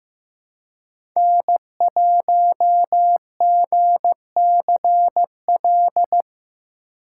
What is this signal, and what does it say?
Morse code, heard as a keyed tone, about 15 wpm, 700 Hz. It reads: N1GCL
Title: USA Call Signs - 5 Characters - 4x Character Spacing - 15wpm